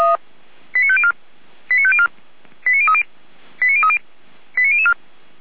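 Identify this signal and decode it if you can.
Five-tone selective call (ZVEI), heard as quick runs of five tones, tones of about 80 ms. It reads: 89504 89504 89030 89030 89004
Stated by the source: ZVEI five-tone call received by SDR, fire brigade Orlamünde (feuerwehr-orlamuende).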